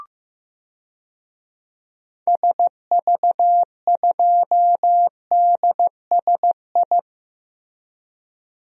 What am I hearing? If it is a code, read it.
Morse code, heard as a keyed tone, about 15 wpm, 700 Hz. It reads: SV2DSI
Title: DXCC-100 Call Signs - 6 Characters - Repeated 3x - 15wpm